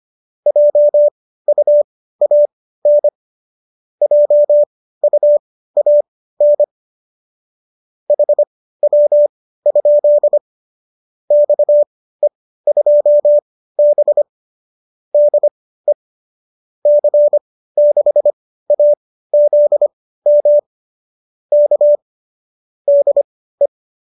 Morse code, heard as a keyed tone, about 25 wpm, spaced as for 15 wpm, 590 Hz. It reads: JUAN JUAN HW? XE2B DE C6AZM K DE
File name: qso204 15